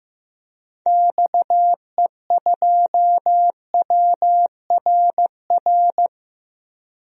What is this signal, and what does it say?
Morse code, heard as a keyed tone, about 15 wpm, 700 Hz. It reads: XE2WRR
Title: DXCC-100 Call Signs - 6 Characters - Repeated 2x - 15wpm